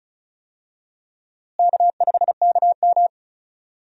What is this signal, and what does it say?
Morse code, heard as a keyed tone, about 36 wpm, 700 Hz. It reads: K5KM
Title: USA Call Signs - 4 Characters - Repeated 3x - 35wpm